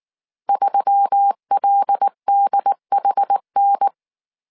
Morse code, heard as a keyed tone, about 19 wpm, 800 Hz. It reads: 3LDHN
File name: CW / DET / 1 6 1500C S C 18wpm 2356